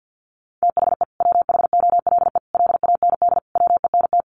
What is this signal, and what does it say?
Morse code, heard as a keyed tone, about 50 wpm, 700 Hz. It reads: THE WHOLE LAND WENT
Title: Sentences from Top 1400 Words 50wpm